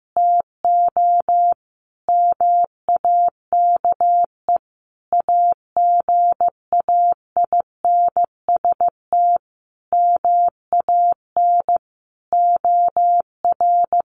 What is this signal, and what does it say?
Morse code, heard as a keyed tone, about 15 wpm, 700 Hz. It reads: TO MAKE AGAINST MAN OR
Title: Sentences from Top 1400 Words 15wpm